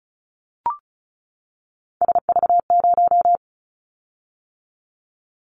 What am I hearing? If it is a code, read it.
Morse code, heard as a keyed tone, about 34 wpm, 700 Hz. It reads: SV0
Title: Callsign Prefixes for Top 50 DXCC - 35wpm